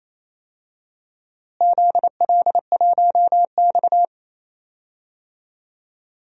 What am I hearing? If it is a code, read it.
Morse code, heard as a keyed tone, about 28 wpm, 700 Hz. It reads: ZL1X
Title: DXCC-100 Call Signs - 4 Characters - Repeated 2x - 28wpm